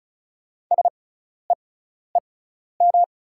Morse code, heard as a keyed tone, about 36 wpm, spaced as for 12 wpm, 700 Hz. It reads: SEEM
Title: 6x Character Spacing - Top 300 Words - 35wpm